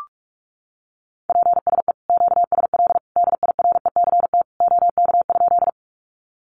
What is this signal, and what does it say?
Morse code, heard as a keyed tone, about 45 wpm, 700 Hz. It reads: PSE QSL DIRECT OK?
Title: Common QSO Phrases 45wpm